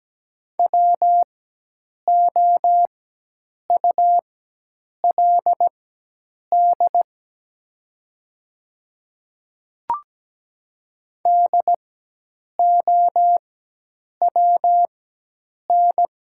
Morse code, heard as a keyed tone, about 17 wpm, spaced as for 8 wpm, 700 Hz. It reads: WOULD DOWN
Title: 4x Character Spacing - Top 300 Words - 17wpm